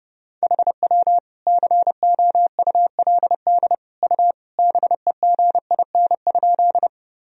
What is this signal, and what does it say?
Morse code, heard as a keyed tone, about 30 wpm, 700 Hz. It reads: HW COULD U BEGIN?
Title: Sentences from Top 300 Words and Abbreviations 30wpm